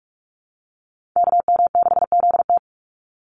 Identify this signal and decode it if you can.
Morse code, heard as a keyed tone, about 44 wpm, 700 Hz. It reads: KM6ZT
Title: Call Signs Repeated 2x - 45wpm